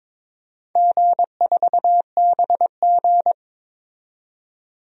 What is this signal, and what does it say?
Morse code, heard as a keyed tone, about 22 wpm, 700 Hz. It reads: G4BG
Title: DXCC-100 Call Signs - 4 Characters - 22wpm